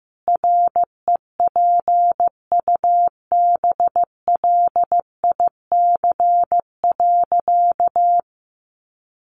Morse code, heard as a keyed tone, about 15 wpm, 700 Hz. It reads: REPUBLIC.
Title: Sentences from Top 1500 Words 15wpm